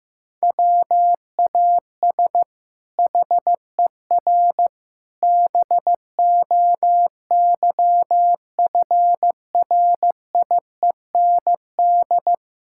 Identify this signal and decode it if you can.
Morse code, heard as a keyed tone, about 16 wpm, 700 Hz. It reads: WAS HER BOYFRIEND
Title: Sentences from Top 1000 Words - Review Part 5 - 15wpm